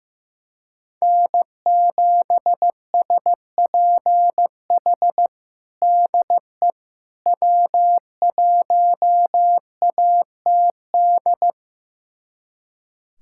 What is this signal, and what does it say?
Morse code, heard as a keyed tone, about 15 wpm, 700 Hz. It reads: N7SPH DE W1ATD
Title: USA Callsign DE Callsign - 5 Characters - 15wpm